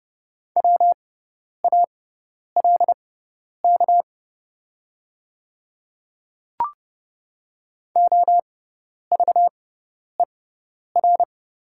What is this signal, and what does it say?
Morse code, heard as a keyed tone, about 30 wpm, spaced as for 10 wpm, 700 Hz. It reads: WALK OVER
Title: 6x Character Spacing - Top 500 Words - 30wpm